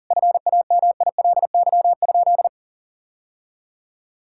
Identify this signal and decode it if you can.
Morse code, heard as a keyed tone, about 40 wpm, 700 Hz. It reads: FAMILY?
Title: Sentences from Top 1000 Words - Review Part 6 - 40wpm